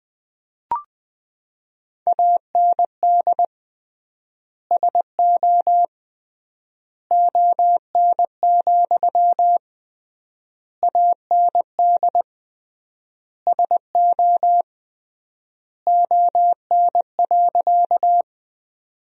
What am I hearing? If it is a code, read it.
Morse code, heard as a keyed tone, about 21 wpm, 700 Hz. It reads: AND SO ON, AND SO ON.
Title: 3x Word-Spacing - Sentences from Top 100 Words - 20wpm